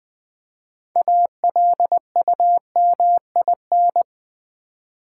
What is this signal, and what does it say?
Morse code, heard as a keyed tone, about 20 wpm, 700 Hz. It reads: ALUMIN
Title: Build A Word - 3 Letters At A Time - 8-Letter Words - 20wpm